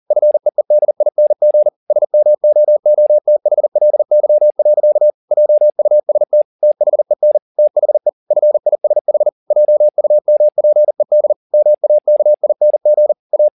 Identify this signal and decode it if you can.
Morse code, heard as a keyed tone, about 40 wpm, 600 Hz. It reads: FEEDING SMOOTHLY. JUST THEN THE FISH JUMPED MAKING A
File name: the old man and the sea 2 40wpm